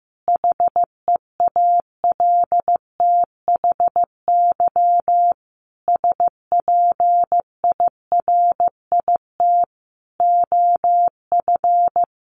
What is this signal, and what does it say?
Morse code, heard as a keyed tone, about 15 wpm, 700 Hz. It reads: HEALTHY SPIRIT OF
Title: Sentences from Top 1400 Words 15wpm